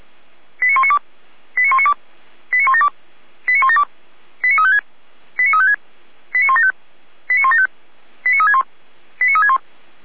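Five-tone selective call (ZVEI), heard as quick runs of five tones, tones of about 70 ms. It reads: 89192 89192 89172 89172 89356 89356 89175 89175 89361 89361
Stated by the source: ZVEI five-tone selective call received on SDR.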